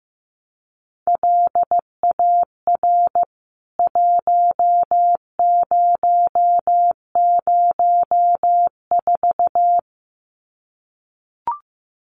Morse code, heard as a keyed tone, about 15 wpm, 700 Hz. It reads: LAR 1004
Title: CWT Exchange 15wpm